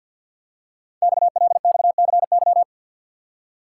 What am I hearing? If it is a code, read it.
Morse code, heard as a keyed tone, about 52 wpm, 700 Hz. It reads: XLXCY
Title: Code Groups - 5 Letters - 50wpm